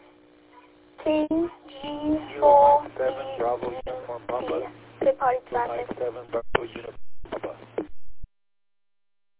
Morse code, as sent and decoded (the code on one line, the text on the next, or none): none